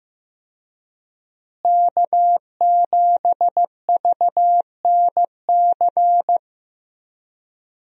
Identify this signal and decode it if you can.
Morse code, heard as a keyed tone, about 15 wpm, 700 Hz. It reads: K7VNC
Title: USA Call Signs - 5 Characters - Repeated 3x - 15wpm